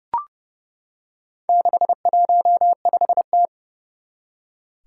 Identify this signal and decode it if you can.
Morse code, heard as a keyed tone, about 30 wpm, 700 Hz. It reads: 615T
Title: Four Challenging Letter-Number Combinations 30wpm